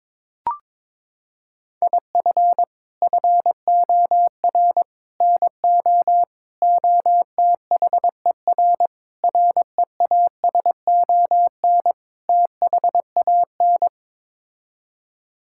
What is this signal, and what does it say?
Morse code, heard as a keyed tone, about 22 wpm, 700 Hz. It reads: IF FOR NO OTHER REASON THAN
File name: practice / 22wpm / Sets of 6 Words - 22wpm